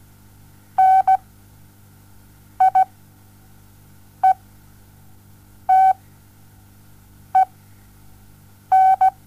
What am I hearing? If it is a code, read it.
Morse code, heard as a keyed tone, about 17 wpm, spaced as for 5 wpm, 760 Hz. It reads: NIETEN
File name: Onlinelehrgang / morsen / DJ4UF Morsen07